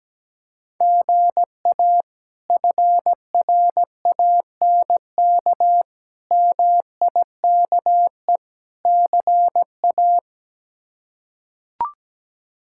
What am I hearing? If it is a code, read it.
Morse code, heard as a keyed tone, about 17 wpm, 700 Hz. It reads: GA FRANK MIKE CA
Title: K1USN SST QSO Practice - 17wpm